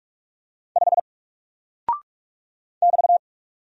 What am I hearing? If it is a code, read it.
Morse code, heard as a keyed tone, about 44 wpm, 700 Hz. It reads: 5 <BT>
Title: Single Character (Letter:Number:Punctuation:Prosign) Alt 45wpm